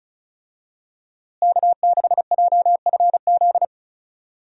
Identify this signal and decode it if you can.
Morse code, heard as a keyed tone, about 36 wpm, 700 Hz. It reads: K6JFZ
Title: USA Call Signs - 5 Characters - Repeated 2x - 35wpm